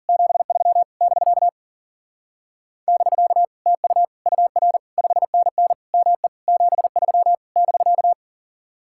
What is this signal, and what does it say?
Morse code, heard as a keyed tone, about 50 wpm, 700 Hz. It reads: 73 <BK> <BK> TU UR 5NN ME 73 <BK>